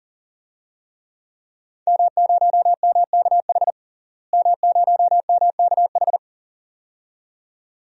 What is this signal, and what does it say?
Morse code, heard as a keyed tone, about 40 wpm, 700 Hz. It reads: M0MKH M0MKH
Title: DXCC-100 Call Signs - 5 Characters - Repeated 2x - 40wpm